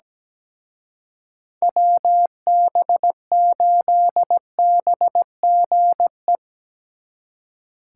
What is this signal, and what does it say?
Morse code, heard as a keyed tone, about 20 wpm, 700 Hz. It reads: WB8BGE
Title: Speed Racing - USA 2x3 Call Signs 17wpm